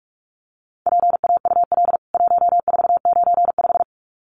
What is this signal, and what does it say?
Morse code, heard as a keyed tone, about 44 wpm, 700 Hz. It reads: PAUL 1495
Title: CWT Exchange 45wpm